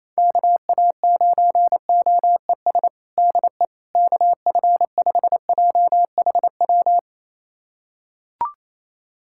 Morse code, heard as a keyed tone, about 28 wpm, 700 Hz. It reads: KA9OES DE KF5JHW
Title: USA Callsign DE Callsign 28wpm